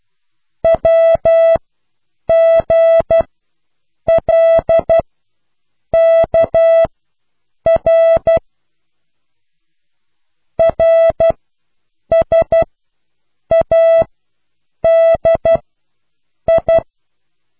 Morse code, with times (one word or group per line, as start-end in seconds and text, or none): 0.6-8.4 WGLKR
10.6-16.8 RSADI